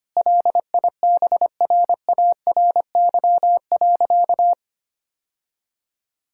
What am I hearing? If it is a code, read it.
Morse code, heard as a keyed tone, about 25 wpm, 700 Hz. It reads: LIBRARY.
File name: practice / 25wpm / Difficult Sentences with Hints - 25wpm